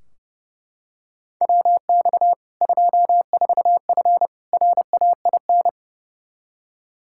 Morse code, heard as a keyed tone, about 32 wpm, 700 Hz. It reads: WX 24F RAIN
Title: K1USN SST QSO Practice 30wpm